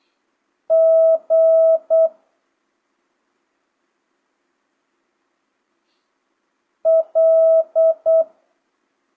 Morse code, heard as a keyed tone, about 8 wpm, 650 Hz. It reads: G L
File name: jh8cbh / 2025cw